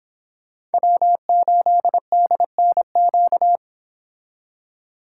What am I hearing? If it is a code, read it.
Morse code, heard as a keyed tone, about 29 wpm, 700 Hz. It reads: W8DNQ